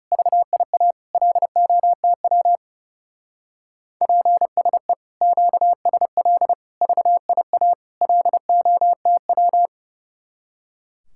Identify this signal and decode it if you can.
Morse code, heard as a keyed tone, about 37 wpm, 700 Hz. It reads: VIA LOTW PSE QSL VIA LOTW
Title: Speed-Racing Common QSO Phrases - 30wpm